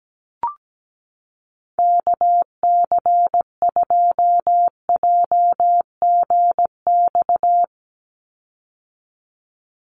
Morse code, heard as a keyed tone, about 17 wpm, 700 Hz. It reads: KC2JGX
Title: USA Call Signs 17wpm